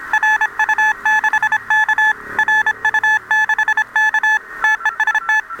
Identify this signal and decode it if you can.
Morse code, heard as a keyed tone, about 26 wpm, 1750 Hz, with noise. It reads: RU6K RU6K TEST